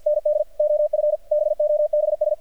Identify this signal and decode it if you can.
Morse code, heard as a keyed tone, about 48 wpm, 600 Hz. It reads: ND OF BOBS
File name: archived / CRSnet